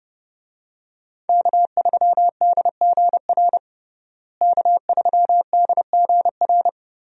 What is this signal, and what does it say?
Morse code, heard as a keyed tone, about 30 wpm, 700 Hz. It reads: K3DGR K3DGR
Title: USA Call Signs - 5 Characters - Repeated 3x - 30wpm